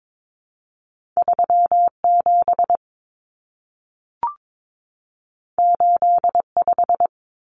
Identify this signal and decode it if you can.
Morse code, heard as a keyed tone, about 22 wpm, 700 Hz. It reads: 37 85
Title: Numbers - 2-Digit 22wpm